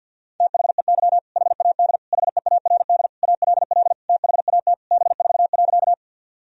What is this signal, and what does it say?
Morse code, heard as a keyed tone, about 50 wpm, 700 Hz. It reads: THEY HAD HEARD ALL THAT B4.
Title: Sentences from Top 400 Words and Abbreviations 50wpm